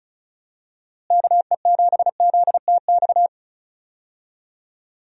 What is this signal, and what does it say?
Morse code, heard as a keyed tone, about 36 wpm, 700 Hz. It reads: KE7ZTX
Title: USA Call Signs - 6 Characters - Repeated 3x - 35wpm